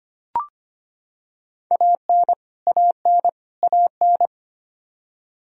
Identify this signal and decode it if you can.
Morse code, heard as a keyed tone, about 25 wpm, 700 Hz. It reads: AN AN AN
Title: Top 100 Words Repeated 25wpm